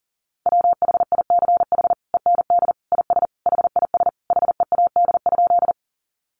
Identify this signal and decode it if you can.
Morse code, heard as a keyed tone, about 37 wpm, 700 Hz. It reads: WHICH END IS HIS HEAD?
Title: Sentences from Top 300 Words 40wpm